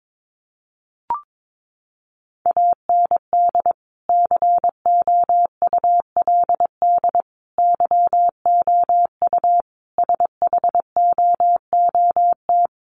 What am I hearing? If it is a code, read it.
Morse code, heard as a keyed tone, about 22 wpm, 700 Hz. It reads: AND COULD YOU SHOOT
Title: Sentences from Top 1200 Words 22wpm